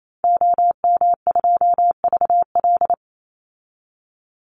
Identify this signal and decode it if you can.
Morse code, heard as a keyed tone, about 28 wpm, 700 Hz. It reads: OM2VL
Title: Call Signs Repeated 2x - 28wpm